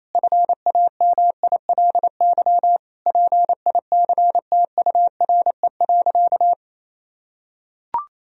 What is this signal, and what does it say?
Morse code, heard as a keyed tone, about 28 wpm, 700 Hz. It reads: FAMILY PICTURE.
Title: Sentences from Top 1000 Words - Review Part 3 - 28wpm